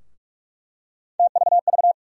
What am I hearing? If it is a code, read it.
Morse code, heard as a keyed tone, about 46 wpm, 700 Hz. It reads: TVV
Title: USA Call Sign Postfixes - 3 Letters 45wpm